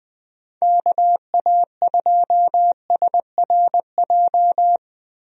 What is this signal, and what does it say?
Morse code, heard as a keyed tone, about 20 wpm, 700 Hz. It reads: KA2SRJ
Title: USA Call Signs - 6 Characters - Repeated 3x - 20wpm